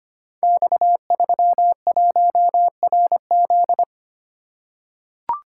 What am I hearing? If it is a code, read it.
Morse code, heard as a keyed tone, about 25 wpm, 700 Hz. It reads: X31RZ
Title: Code Groups - 5 Letter:Numbers - 25wpm